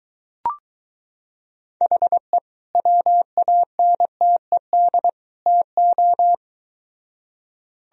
Morse code, heard as a keyed tone, about 23 wpm, 700 Hz. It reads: HE WANTED TO